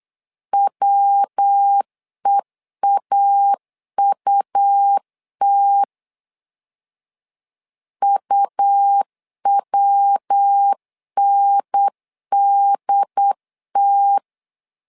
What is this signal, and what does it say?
Morse code, heard as a keyed tone, about 8 wpm, 800 Hz. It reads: WEAUT UWNDT
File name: LEZIONI / 3 MWRO / 3 MW 8wpm 2501